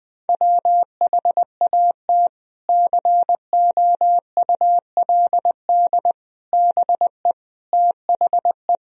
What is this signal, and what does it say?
Morse code, heard as a keyed tone, about 20 wpm, 700 Hz. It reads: WHAT COULD BE THE
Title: Sentences from Top 1000 Words - Review Part 1 - 20wpm